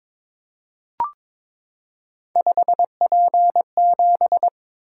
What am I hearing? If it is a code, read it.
Morse code, heard as a keyed tone, about 22 wpm, 700 Hz. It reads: 5P7
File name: practice / 22wpm / Three Letter-Number Combinations - 22wpm